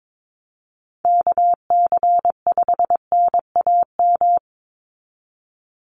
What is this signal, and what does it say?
Morse code, heard as a keyed tone, about 22 wpm, 700 Hz. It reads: KC5NAM
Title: USA Call Signs 22wpm